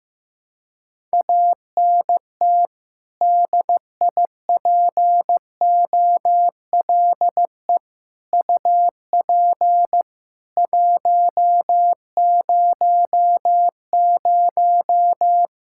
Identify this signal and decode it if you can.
Morse code, heard as a keyed tone, about 15 wpm, 700 Hz. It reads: ANT DIPOLE UP 100
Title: Practice QSO - Exchange 2 - Info - 15wpm